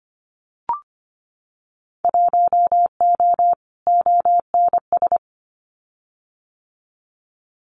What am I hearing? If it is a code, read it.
Morse code, heard as a keyed tone, about 25 wpm, 700 Hz. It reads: 1O ONS
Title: Winter Field Day Exchanges 25wpm